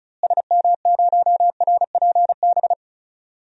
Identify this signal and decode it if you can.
Morse code, heard as a keyed tone, about 36 wpm, 700 Hz. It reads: SM0RPB
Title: DXCC-100 Call Signs - 6 Characters - Repeated 3x - 35wpm